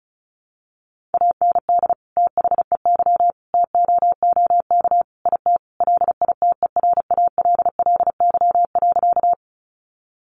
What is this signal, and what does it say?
Morse code, heard as a keyed tone, about 34 wpm, 700 Hz. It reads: AND THEY TOOK IT LITERALLY.
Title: Sentences from Top 1400 Words 35wpm